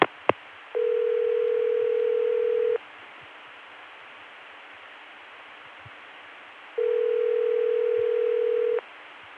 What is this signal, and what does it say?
Ringback tone (North America), 440 + 480 Hz.